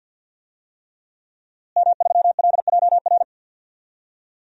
Morse code, heard as a keyed tone, about 51 wpm, 700 Hz. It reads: M3LJR
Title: DXCC-100 Call Signs - 5 Characters - Repeated 3x - 50wpm